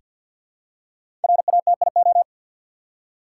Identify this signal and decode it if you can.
Morse code, heard as a keyed tone, about 48 wpm, 700 Hz. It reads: RATIO
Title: Five Letter Words - 50wpm